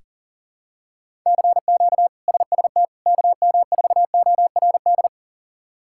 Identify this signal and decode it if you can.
Morse code, heard as a keyed tone, about 40 wpm, 700 Hz. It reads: CQ SST KM4ORD
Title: K1USN SST QSO Practice - 40wpm